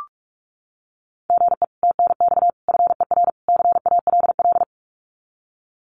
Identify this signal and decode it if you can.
Morse code, heard as a keyed tone, about 46 wpm, 700 Hz. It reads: GE TNX FER CALL
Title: Common QSO Phrases Encore 45wpm